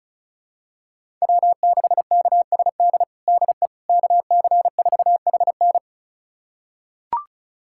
Morse code, heard as a keyed tone, about 36 wpm, 700 Hz. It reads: W6KSD DE KC4HN
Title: USA Callsign DE Callsign - 5 Characters - 35wpm